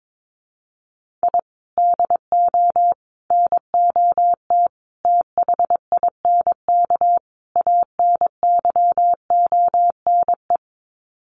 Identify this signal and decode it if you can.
Morse code, heard as a keyed tone, about 22 wpm, 700 Hz. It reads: I DO NOT THINK ANYONE
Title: Sets of 5 Words Encore 22wpm